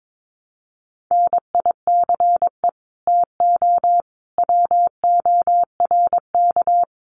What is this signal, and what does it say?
Morse code, heard as a keyed tone, about 22 wpm, 700 Hz. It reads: NICE TO WORK
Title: Common QSO Phrases 22wpm